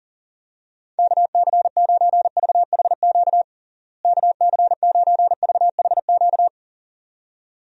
Morse code, heard as a keyed tone, about 40 wpm, 700 Hz. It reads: KC9VHQ KC9VHQ
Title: USA Call Signs - 6 Characters - Repeated 2x - 40wpm